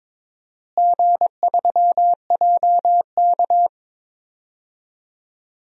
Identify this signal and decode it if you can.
Morse code, heard as a keyed tone, about 22 wpm, 700 Hz. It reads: G3JK
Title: DXCC-100 Call Signs - 4 Characters - Repeated 2x - 22wpm